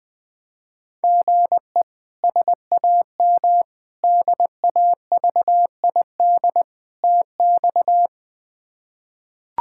Morse code, heard as a keyed tone, about 21 wpm, 700 Hz. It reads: GE SAM DAVID TX
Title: K1USN SST QSO Practice - 20wpm